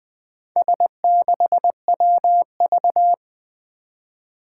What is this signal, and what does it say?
Morse code, heard as a keyed tone, about 21 wpm, 700 Hz. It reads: S6WV
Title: Four Challenging Letter-Number Combinations 20wpm